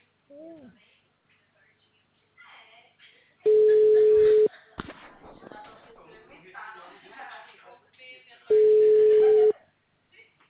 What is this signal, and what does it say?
Ringback tone (Europe, 425 Hz), with speech.